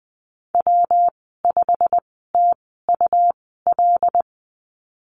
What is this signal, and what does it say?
Morse code, heard as a keyed tone, about 20 wpm, spaced as for 14 wpm, 700 Hz. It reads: W5TUL
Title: 173 Lesson 28g - Number 9 - Call Signs 20wpm